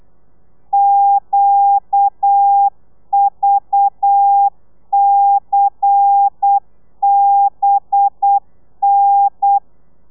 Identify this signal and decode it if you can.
Morse code, heard as a keyed tone, about 8 wpm, 800 Hz. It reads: QVCBN
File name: telegrafia / i2tfj / Lez10 35